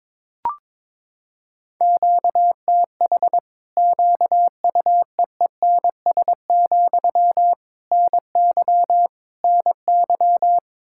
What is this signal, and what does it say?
Morse code, heard as a keyed tone, about 22 wpm, 700 Hz. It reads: QTH QUEENS, NY NY
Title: Practice QSO - Exchange 1 - Beginning - 22wpm